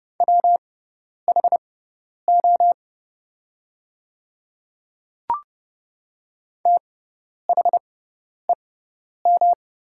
Morse code, heard as a keyed tone, about 32 wpm, spaced as for 10 wpm, 690 Hz. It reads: WHO THEM
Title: 6x Character Spacing - Top 200 Words - 30wpm